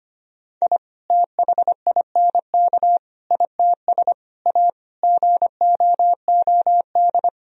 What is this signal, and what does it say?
Morse code, heard as a keyed tone, about 25 wpm, 700 Hz. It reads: I THINK ITS A GOOD
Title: Sentences from Top 1000 Words - Review Part 1 - 25wpm